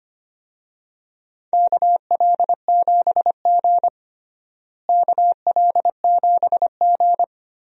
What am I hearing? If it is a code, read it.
Morse code, heard as a keyed tone, about 25 wpm, 700 Hz. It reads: KL7G KL7G
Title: DXCC-100 Call Signs - 4 Characters - Repeated 3x - 25wpm